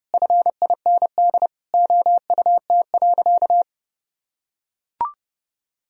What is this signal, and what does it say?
Morse code, heard as a keyed tone, about 32 wpm, 700 Hz. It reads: FIND OUT.
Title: Sentences from Top 200 Words and Abbreviations 30wpm